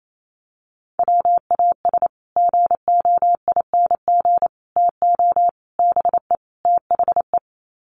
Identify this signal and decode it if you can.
Morse code, heard as a keyed tone, about 28 wpm, 700 Hz. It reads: WAS GOING TO BE THE
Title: Sets of 5 Words Encore2 28wpm